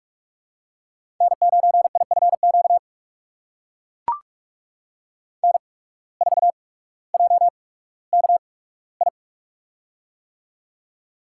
Morse code, heard as a keyed tone, about 44 wpm, spaced as for 19 wpm, 700 Hz. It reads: N9IFQ N4JKI
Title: USA Call Signs - 5 Characters - 8x Character Spacing - 45wpm